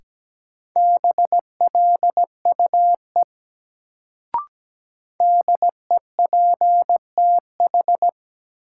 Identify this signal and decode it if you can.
Morse code, heard as a keyed tone, about 17 wpm, 700 Hz. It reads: BLUE DEPTH
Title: Sentences from Top 1200 Words - New Word Drill Alt - 17wpm